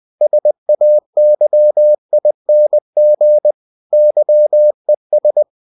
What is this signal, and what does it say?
Morse code, heard as a keyed tone, about 20 wpm, 600 Hz. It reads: SAYING YES
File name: cw-books / winnie-the-pooh / winnie-20wpm-page-0031